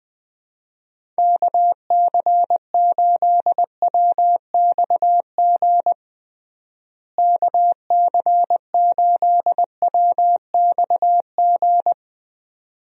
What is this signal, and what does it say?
Morse code, heard as a keyed tone, about 20 wpm, 700 Hz. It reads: KC8WXG KC8WXG
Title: USA Call Signs - 6 Characters - Repeated 2x - 20wpm